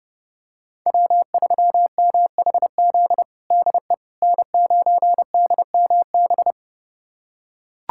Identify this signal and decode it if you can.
Morse code, heard as a keyed tone, about 30 wpm, 700 Hz. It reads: W3MHZ DE N9DMB
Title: USA Callsign DE Callsign - 5 Characters - 30wpm